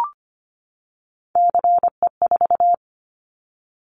Morse code, heard as a keyed tone, about 25 wpm, 700 Hz. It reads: CE4
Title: Callsign Prefixes for Top 100 DXCC - 25wpm